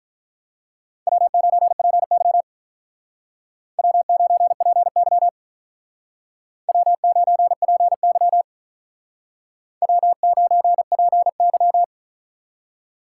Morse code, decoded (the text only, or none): W9PY W9PY W9PY W9PY